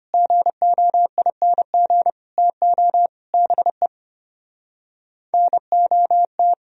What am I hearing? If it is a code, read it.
Morse code, heard as a keyed tone, about 33 wpm, 700 Hz. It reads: GOING TO BE NOT